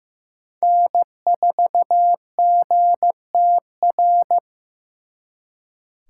Morse code, heard as a keyed tone, about 18 wpm, 700 Hz. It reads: N4GTR
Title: Speed Racing - USA 1x3 Call Signs 15wpm